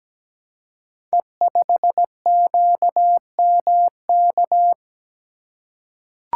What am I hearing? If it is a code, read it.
Morse code, heard as a keyed tone, about 17 wpm, 700 Hz. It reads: E5QMK
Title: Code Groups - 5 Letter:Numbers - 17wpm